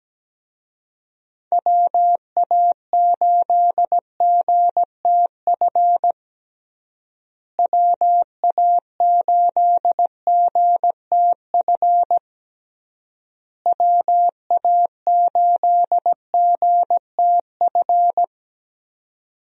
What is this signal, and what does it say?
Morse code, heard as a keyed tone, about 17 wpm, 700 Hz. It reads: WA8GTF WA8GTF WA8GTF
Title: USA Call Signs - 6 Characters - Repeated 3x - 17wpm